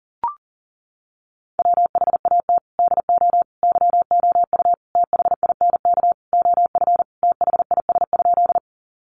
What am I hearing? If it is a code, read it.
Morse code, heard as a keyed tone, about 37 wpm, 700 Hz. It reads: WHAT DO YOU THINK OF THIS?
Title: Sentences from Top 200 Words 40wpm